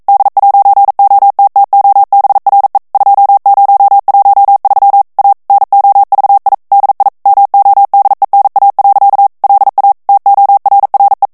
Morse code, heard as a keyed tone, about 42 wpm, 800 Hz. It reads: D'OTTOBRE 2013 A NOVI DI MODENA. LA TORRE
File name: ITALIEN texte clair 1 vitesse 40 mots minute